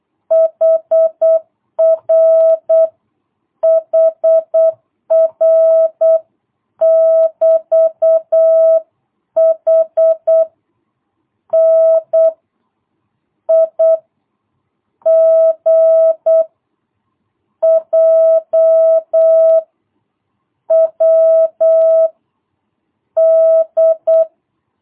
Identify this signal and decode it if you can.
Morse code, heard as a keyed tone, about 8 wpm, 660 Hz. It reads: HR HR <BT> H N I G J W D
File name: jh8cbh / 2022cw